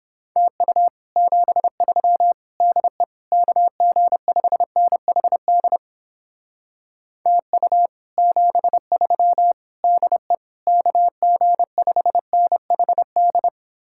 Morse code, heard as a keyed tone, about 29 wpm, 700 Hz. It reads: TU 73 DE KG5NHD TU 73 DE KG5NHD